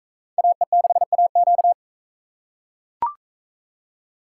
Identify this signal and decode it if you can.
Morse code, heard as a keyed tone, about 42 wpm, 700 Hz. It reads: AE6AQ